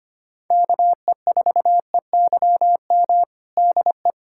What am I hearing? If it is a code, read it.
Morse code, heard as a keyed tone, about 25 wpm, 700 Hz. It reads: KE4EYM DE
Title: USA Callsign DE Callsign - 6 Characters - 25wpm